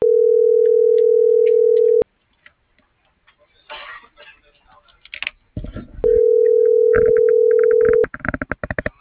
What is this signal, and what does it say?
Ringback tone (North America), 440 + 480 Hz.